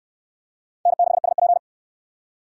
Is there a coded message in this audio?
Yes: Morse code from 0.9 to 1.6 s, heard as a keyed tone, about 67 wpm, 700 Hz. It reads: N6SP